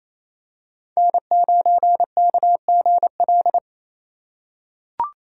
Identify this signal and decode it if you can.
Morse code, heard as a keyed tone, about 28 wpm, 700 Hz. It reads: N9KGL
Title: USA Call Signs - 3x Character Spacing - 28wpm